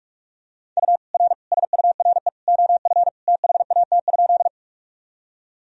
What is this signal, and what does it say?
Morse code, heard as a keyed tone, about 44 wpm, 700 Hz. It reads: U R SURE OF THAT?
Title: Sentences from Top 300 Words and Abbreviations 45wpm